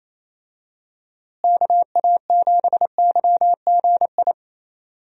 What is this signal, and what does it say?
Morse code, heard as a keyed tone, about 28 wpm, 700 Hz. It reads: KA7YGI
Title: USA Call Signs - 6 Characters - Repeated 3x - 28wpm